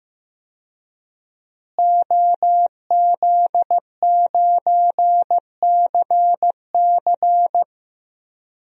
Morse code, heard as a keyed tone, about 15 wpm, 700 Hz. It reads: OZ9CC
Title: DXCC-100 Call Signs - 5 Characters - Repeated 2x - 15wpm